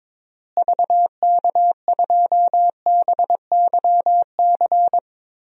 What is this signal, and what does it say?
Morse code, heard as a keyed tone, about 22 wpm, 700 Hz. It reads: VK2BYC
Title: DXCC-100 Call Signs - 6 Characters - Repeated 3x - 22wpm